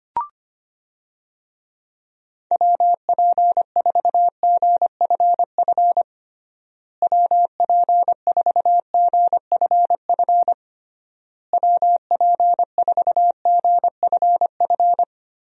Morse code, heard as a keyed tone, about 25 wpm, 700 Hz. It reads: WP4GFF WP4GFF WP4GFF
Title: USA Call Signs - 6 Characters - Repeated 3x - 25wpm